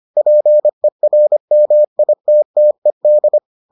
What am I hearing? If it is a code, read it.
Morse code, heard as a keyed tone, about 25 wpm, 600 Hz. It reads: PERMITTED